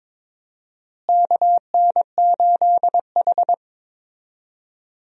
Morse code, heard as a keyed tone, about 21 wpm, 700 Hz. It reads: KN8H
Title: USA Call Signs - 4 Characters - Repeated 2x - 22wpm